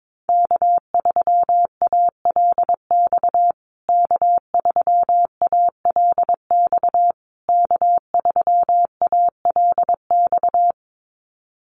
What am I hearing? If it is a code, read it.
Morse code, heard as a keyed tone, about 22 wpm, 700 Hz. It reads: K3ALX K3ALX K3ALX
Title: USA Call Signs Repeated 3x 22wpm